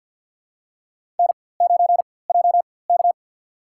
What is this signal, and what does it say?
Morse code, heard as a keyed tone, about 47 wpm, spaced as for 23 wpm, 700 Hz. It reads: N9JK
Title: USA Call Signs - 4x Character Spacing - 50wpm